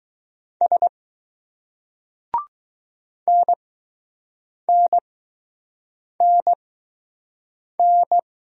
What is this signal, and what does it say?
Morse code, heard as a keyed tone, about 20 wpm, spaced as for 6 wpm, 700 Hz. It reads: S NNNN